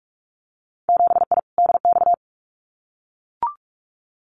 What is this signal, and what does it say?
Morse code, heard as a keyed tone, about 44 wpm, 700 Hz. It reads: 7I DX
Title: Winter Field Day Exchanges Encore 45wpm